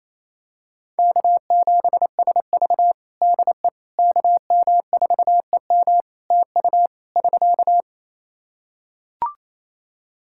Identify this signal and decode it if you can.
Morse code, heard as a keyed tone, about 28 wpm, 700 Hz. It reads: K7SV DE KM4EM TU <SK>
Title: Practice QSO - Exchange 4 - Ending - 28wpm